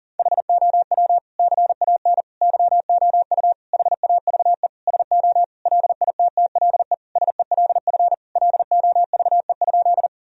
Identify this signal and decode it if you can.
Morse code, heard as a keyed tone, about 40 wpm, 700 Hz. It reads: HOW CAN YOU HAVE SO LITTLE SELF LOVE?
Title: Sentences from Top 1000 Words - Review Part 1 - 40wpm